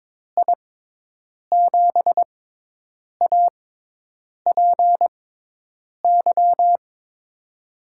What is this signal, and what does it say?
Morse code, heard as a keyed tone, about 22 wpm, spaced as for 8 wpm, 700 Hz. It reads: I7APY
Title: USA Call Signs - 6 Characters - 6x Character Spacing - 22wpm